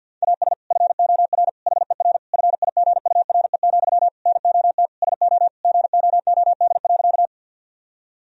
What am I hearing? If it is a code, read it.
Morse code, heard as a keyed tone, about 50 wpm, 700 Hz. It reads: AS FOR HER FIGURE, NOT SO GOOD.
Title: Sentences from Top 1000 Words - Review Part 2 - 50wpm